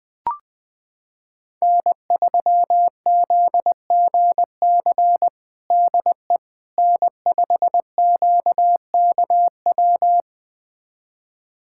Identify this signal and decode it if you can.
Morse code, heard as a keyed tone, about 20 wpm, 700 Hz. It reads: N3ZGC DE N5QKW
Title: USA Callsign DE Callsign - 5 Characters - 20wpm